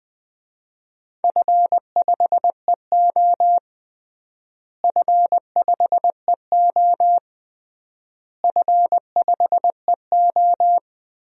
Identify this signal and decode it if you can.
Morse code, heard as a keyed tone, about 20 wpm, 700 Hz. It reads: F5EO F5EO F5EO
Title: DXCC-100 Call Signs - 4 Characters - Repeated 3x - 20wpm